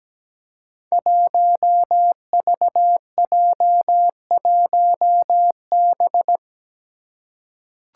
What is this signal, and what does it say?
Morse code, heard as a keyed tone, about 17 wpm, 700 Hz. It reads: 1VJ1B
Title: Five Challenging Letter-Number Combinations 17wpm